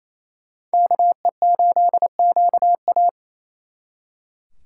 Morse code, heard as a keyed tone, about 36 wpm, 700 Hz. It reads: KE8QA